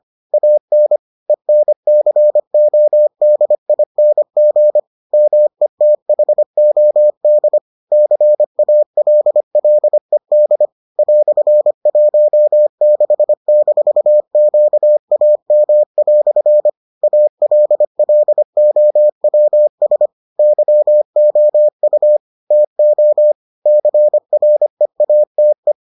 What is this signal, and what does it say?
Morse code, heard as a keyed tone, about 25 wpm, 600 Hz. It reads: AN ENCODING METHOD CALLED "16-QAM" ALLOWS YOU TO CREATE